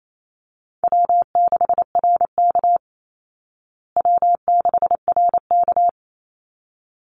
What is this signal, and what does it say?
Morse code, heard as a keyed tone, about 28 wpm, 700 Hz. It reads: W6RK W6RK
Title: Call Signs Repeated 3x - 28wpm